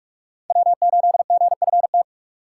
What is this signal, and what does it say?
Morse code, heard as a keyed tone, about 44 wpm, 700 Hz. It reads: W8GFT